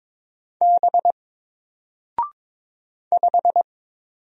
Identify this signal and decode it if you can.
Morse code, heard as a keyed tone, about 22 wpm, 700 Hz. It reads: B 5
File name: Single-Letter-Number-22wpm